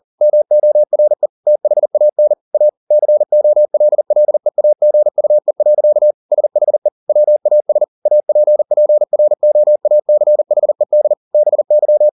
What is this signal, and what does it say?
Morse code, heard as a keyed tone, about 40 wpm, 600 Hz. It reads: MORE THAN A COLLEAGUE. SHE WAS APPROACHED BY